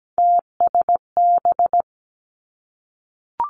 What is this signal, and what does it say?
Morse code, heard as a keyed tone, about 17 wpm, 700 Hz. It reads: TSB
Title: Three Challenging Letter-Number Combinations 17wpm